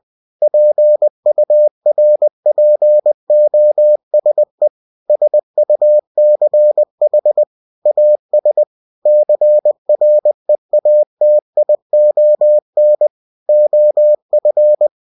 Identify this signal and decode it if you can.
Morse code, heard as a keyed tone, about 20 wpm, 600 Hz. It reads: PURPOSE SUCH AS CREATION OF
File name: cw-books / winnie-the-pooh / winnie-20wpm-page-0112